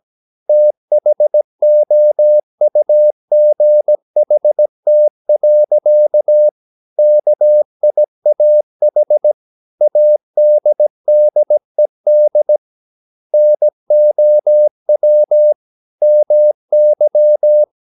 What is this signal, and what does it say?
Morse code, heard as a keyed tone, about 17 wpm, 600 Hz. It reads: THOUGHT. KIAH ADDED NOW MY
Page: Romanen Never af Ken Follet på engelsk.